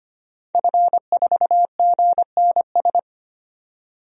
Morse code, heard as a keyed tone, about 25 wpm, 700 Hz. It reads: F4GNS
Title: DXCC-100 Call Signs - 5 Characters - 25wpm